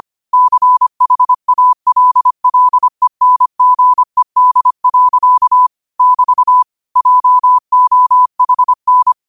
Morse code, heard as a keyed tone, about 25 wpm, 1000 Hz. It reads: CHALLENGED. <BT> JOHN
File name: qotdfiles / 25wpm